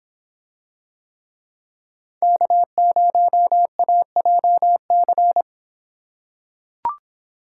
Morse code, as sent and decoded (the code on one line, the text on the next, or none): -.- ----- .- .--- -.-.
K0AJC